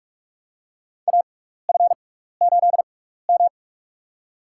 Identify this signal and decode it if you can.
Morse code, heard as a keyed tone, about 46 wpm, spaced as for 16 wpm, 700 Hz. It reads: AF8M
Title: USA Call Signs - 6x Character Spacing - 45wpm